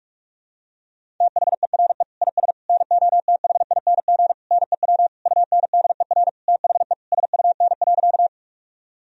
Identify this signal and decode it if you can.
Morse code, heard as a keyed tone, about 42 wpm, 700 Hz. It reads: THERE IS NOTHING NEW UNDER THE SUN.
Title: Sentences from Top 1000 Words - Review Part 1 - 45wpm